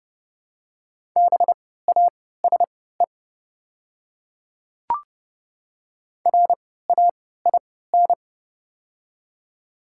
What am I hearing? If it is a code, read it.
Morse code, heard as a keyed tone, about 30 wpm, spaced as for 17 wpm, 700 Hz. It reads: BASE RAIN
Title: 3x Character Spacing - Top 500 Words - 30wpm